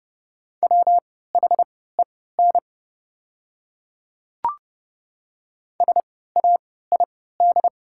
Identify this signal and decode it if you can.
Morse code, heard as a keyed tone, about 30 wpm, spaced as for 17 wpm, 690 Hz. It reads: WHEN SAID
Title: 3x Character Spacing - Top 200 Words - 30wpm